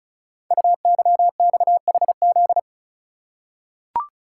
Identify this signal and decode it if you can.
Morse code, heard as a keyed tone, about 36 wpm, 690 Hz. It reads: UYXHZ
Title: Code Groups - 5 Letters - 35wpm